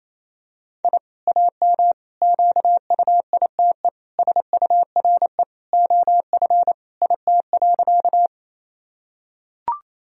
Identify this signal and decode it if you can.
Morse code, heard as a keyed tone, about 28 wpm, 700 Hz. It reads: I AM QUITE SURE OF IT.
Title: Sentences from Top 1000 Words - Review Part 2 - 28wpm